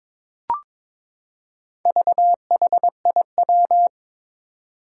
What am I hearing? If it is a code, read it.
Morse code, heard as a keyed tone, about 21 wpm, 700 Hz. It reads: VHIW
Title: Four Challenging Letter-Number Combinations 22wpm